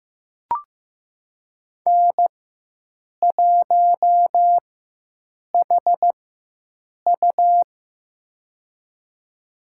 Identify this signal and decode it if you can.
Morse code, heard as a keyed tone, about 15 wpm, spaced as for 7 wpm, 700 Hz. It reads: N1HU
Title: USA Call Signs - 4x Character Spacing - 15wpm